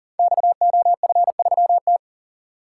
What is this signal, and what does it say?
Morse code, heard as a keyed tone, about 43 wpm, 700 Hz. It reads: XOF3T
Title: Code Groups - 5 Letter:Numbers - 40wpm